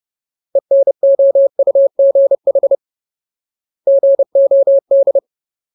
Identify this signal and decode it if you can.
Morse code, heard as a keyed tone, about 30 wpm, 550 Hz. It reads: ENOUGH GOD
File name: Ultimate Word List / 30WPM Ultimate